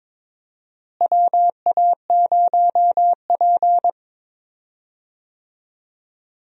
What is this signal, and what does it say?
Morse code, heard as a keyed tone, about 22 wpm, 700 Hz. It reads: WA0P
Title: USA Call Signs - 4 Characters - Repeated 2x - 22wpm